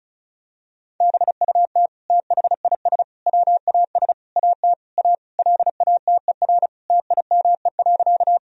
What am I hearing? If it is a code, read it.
Morse code, heard as a keyed tone, about 36 wpm, 700 Hz. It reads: BUT THIS WAS AT A LATER TIME.
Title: Sentences from Top 1000 Words - Review Part 4 - 35wpm